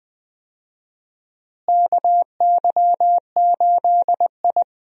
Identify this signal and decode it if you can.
Morse code, heard as a keyed tone, about 20 wpm, 700 Hz. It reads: KY8I
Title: USA Call Signs - 4 Characters - Repeated 3x - 20wpm